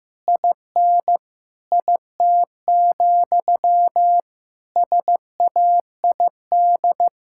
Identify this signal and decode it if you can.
Morse code, heard as a keyed tone, about 16 wpm, 700 Hz. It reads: IN IT, SAID
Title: Sentences from Top 1000 Words - Review Part 5 - 15wpm